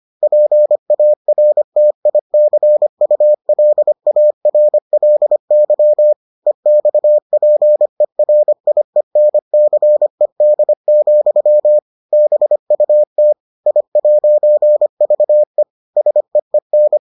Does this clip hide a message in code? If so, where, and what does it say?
Yes: Morse code from 0.2 to 17.0 s, heard as a keyed tone, about 25 wpm, 600 Hz. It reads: PARTICULARLY EXPERIENCED, BUT I'VE SEEN